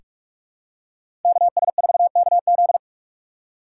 Morse code, heard as a keyed tone, about 45 wpm, 700 Hz. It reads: KS4KZ
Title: USA Call Signs - 5 Characters - 4x Character Spacing - 45wpm